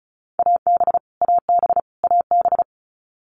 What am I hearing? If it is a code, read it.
Morse code, heard as a keyed tone, about 34 wpm, 700 Hz. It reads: AB AB AB
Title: Canadian Provinces and Territories - Repeated 3x - 35wpm